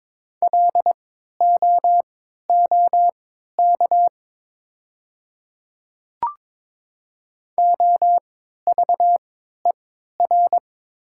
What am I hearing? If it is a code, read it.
Morse code, heard as a keyed tone, about 22 wpm, spaced as for 13 wpm, 700 Hz. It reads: LOOK OVER
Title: 3x Character Spacing - Top 100 Words - 22wpm